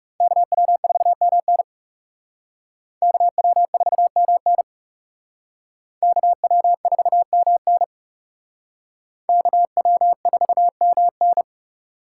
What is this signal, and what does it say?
Morse code, heard as a keyed tone, about 37 wpm, 700 Hz. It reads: KW4MN KW4MN KW4MN KW4MN
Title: Speed Racing - USA 2x2 Call Signs 30wpm